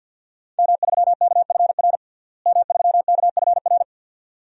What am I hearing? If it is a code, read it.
Morse code, heard as a keyed tone, about 51 wpm, 700 Hz. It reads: M3KFR M3KFR
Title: DXCC-100 Call Signs - 5 Characters - Repeated 2x - 50wpm